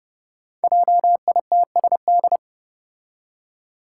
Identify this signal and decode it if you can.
Morse code, heard as a keyed tone, about 30 wpm, 700 Hz. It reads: JITSD
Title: Five Challenging Letter-Number Combinations 30wpm